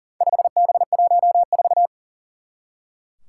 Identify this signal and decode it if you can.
Morse code, heard as a keyed tone, about 40 wpm, 700 Hz. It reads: 5B14
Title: Four Challenging Letter-Number Combinations 40wpm